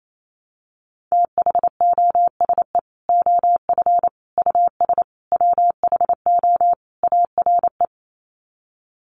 Morse code, heard as a keyed tone, about 28 wpm, 700 Hz. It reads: THOSE OF US WHO ARE
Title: Sets of 5 Words Encore2 28wpm